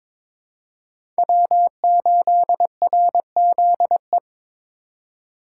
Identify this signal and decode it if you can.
Morse code, heard as a keyed tone, about 22 wpm, 700 Hz. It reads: W8RZE
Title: USA Call Signs - 2x Character Spacing - 22wpm